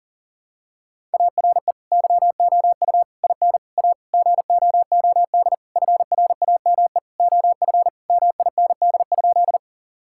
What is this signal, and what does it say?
Morse code, heard as a keyed tone, about 40 wpm, 700 Hz. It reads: ARE YOU IN A GOOD FRAME OF MIND?
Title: Sentences from Top 1000 Words - Review Part 6 - 40wpm